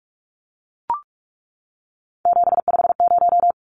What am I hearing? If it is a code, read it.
Morse code, heard as a keyed tone, about 44 wpm, 700 Hz. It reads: 750
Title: Numbers - 3-Digit 45wpm